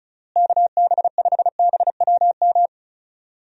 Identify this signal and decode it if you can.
Morse code, heard as a keyed tone, about 34 wpm, 700 Hz. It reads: KB5BWM
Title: USA Call Signs - 6 Characters - Repeated 2x - 35wpm